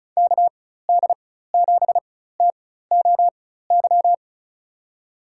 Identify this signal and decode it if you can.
Morse code, heard as a keyed tone, about 34 wpm, spaced as for 16 wpm, 700 Hz. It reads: KD7TOY
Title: USA Call Signs - 4x Character Spacing - 35wpm